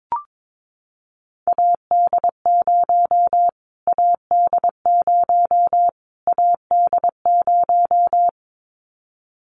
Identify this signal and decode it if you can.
Morse code, heard as a keyed tone, about 22 wpm, 700 Hz. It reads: AD0 AD0 AD0
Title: Callsign Prefixes for Top 25 DXCC - Repeated 3x - 22wpm